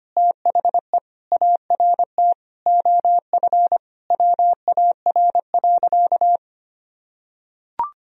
Morse code, heard as a keyed tone, about 25 wpm, 700 Hz. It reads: THE ART OF WAR.
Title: Sentences from Top 1000 Words - Review Part 3 - 25wpm